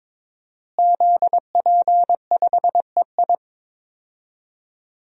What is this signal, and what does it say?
Morse code, heard as a keyed tone, about 22 wpm, 700 Hz. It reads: ZP5EI
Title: DXCC-100 Call Signs - 5 Characters - Repeated 2x - 22wpm